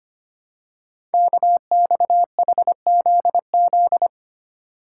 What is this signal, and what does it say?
Morse code, heard as a keyed tone, about 25 wpm, 700 Hz. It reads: KXHZZ
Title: Code Groups - 5 Letters - 25wpm